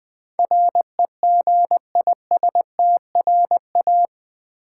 Morse code, heard as a keyed tone, about 20 wpm, 700 Hz. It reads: REGISTRA
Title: Build A Word - 4 Letters At A Time - 12-Letter Words - 20wpm